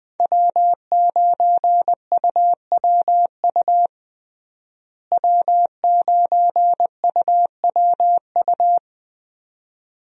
Morse code, heard as a keyed tone, about 21 wpm, 700 Hz. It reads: W9UWU W9UWU
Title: USA Call Signs - 5 Characters - Repeated 2x - 20wpm